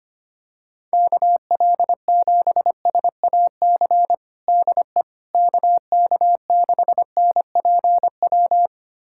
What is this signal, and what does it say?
Morse code, heard as a keyed tone, about 25 wpm, 700 Hz. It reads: KL7SAC DE KK6NPW
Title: USA Callsign DE Callsign - 6 Characters - 25wpm